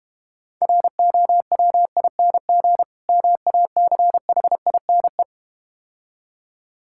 Morse code, heard as a keyed tone, about 32 wpm, 700 Hz. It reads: ROWING MACHINE